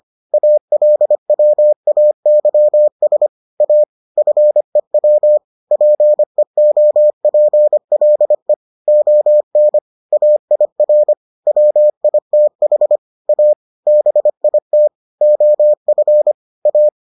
Morse code, heard as a keyed tone, about 25 wpm, 600 Hz. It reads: ALWAYS A FEW PEOPLE ON AIR WITH A BIT OF A